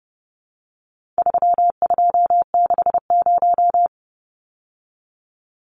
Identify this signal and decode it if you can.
Morse code, heard as a keyed tone, about 30 wpm, 700 Hz. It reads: 3260
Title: Numbers - 4-Digit 30wpm